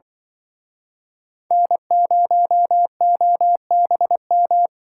Morse code, heard as a keyed tone, about 24 wpm, 700 Hz. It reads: N0OBM
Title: Speed Racing - USA 1x3 Call Signs 20wpm